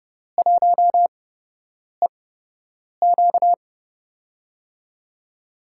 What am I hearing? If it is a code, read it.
Morse code, heard as a keyed tone, about 30 wpm, spaced as for 8 wpm, 700 Hz. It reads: 1EQ
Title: USA Call Signs - 4 Characters - 8x Character Spacing - 30wpm